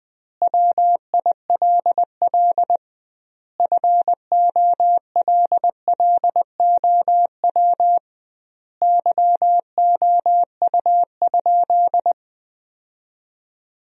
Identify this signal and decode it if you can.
Morse code, heard as a keyed tone, about 20 wpm, 700 Hz. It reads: WILL FOLLOW YOU?
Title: 2x Word-Spacing - Sentences from Top 200 Words - 20wpm